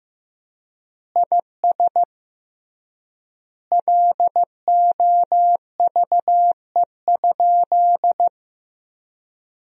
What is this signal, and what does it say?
Morse code, heard as a keyed tone, about 15 wpm, 700 Hz. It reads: IS LOVE?
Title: 3x Word-Spacing - Sentences from Top 400 Words - 15wpm